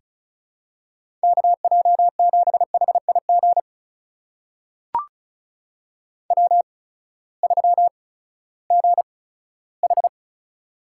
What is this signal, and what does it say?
Morse code, heard as a keyed tone, about 36 wpm, spaced as for 18 wpm, 700 Hz. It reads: KJ7HIG W3GH
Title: USA Call Signs - 8x Character Spacing - 35wpm